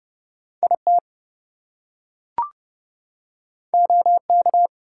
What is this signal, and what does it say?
Morse code, heard as a keyed tone, about 30 wpm, 700 Hz. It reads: IT OK
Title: Two Letter Words - 30wpm